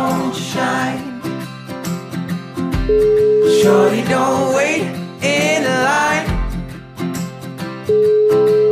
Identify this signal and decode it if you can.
Ringback tone (Europe, 425 Hz), with noise.